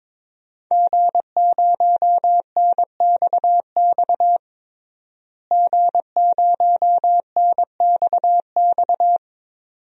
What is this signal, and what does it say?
Morse code, heard as a keyed tone, about 22 wpm, 700 Hz. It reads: G0NXX G0NXX
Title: DXCC-100 Call Signs - 5 Characters - Repeated 2x - 22wpm